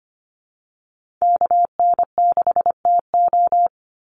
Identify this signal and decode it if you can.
Morse code, heard as a keyed tone, about 25 wpm, 700 Hz. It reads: KN6TO
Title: USA Call Signs 25wpm